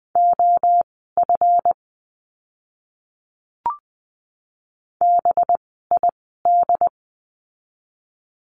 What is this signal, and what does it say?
Morse code, heard as a keyed tone, about 20 wpm, spaced as for 15 wpm, 700 Hz. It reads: OF BID
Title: 185 Lesson 30f - Letter b - Words 20wpm